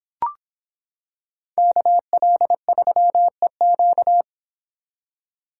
Morse code, heard as a keyed tone, about 25 wpm, 700 Hz. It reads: KL3EQ